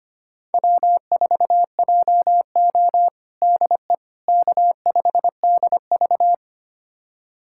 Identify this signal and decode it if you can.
Morse code, heard as a keyed tone, about 25 wpm, 700 Hz. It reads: W4JO DE K5DV
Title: USA Callsign DE Callsign - 4 Characters - 25wpm